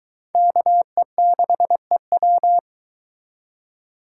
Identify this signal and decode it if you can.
Morse code, heard as a keyed tone, about 22 wpm, 700 Hz. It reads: KE6EW